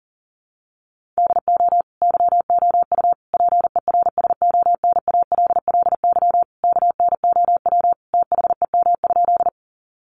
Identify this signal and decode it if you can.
Morse code, heard as a keyed tone, about 40 wpm, 700 Hz. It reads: DO YOU PERSONALLY KNOW THEM?
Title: Sentences from Top 1400 Words 40wpm